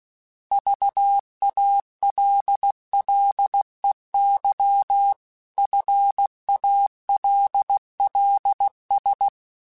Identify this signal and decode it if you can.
Morse code, heard as a keyed tone, about 16 wpm, 800 Hz. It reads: VALLEY FALLS
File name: fromSMs / FromSMs mp3s / Cities@15